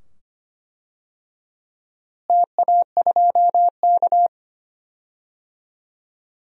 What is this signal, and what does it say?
Morse code, heard as a keyed tone, about 25 wpm, 700 Hz. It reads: TA2K
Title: DXCC-100 Call Signs - 4 Characters - Repeated 2x - 25wpm